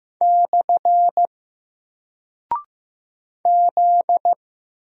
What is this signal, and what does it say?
Morse code, heard as a keyed tone, about 15 wpm, 700 Hz. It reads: / Z
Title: Single Character (Letter:Number:Punctuation:Prosign) 15wpm